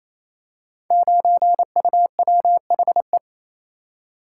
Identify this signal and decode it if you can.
Morse code, heard as a keyed tone, about 28 wpm, 700 Hz. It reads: 9UWHE
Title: Code Groups - 5 Letter:Numbers - 28wpm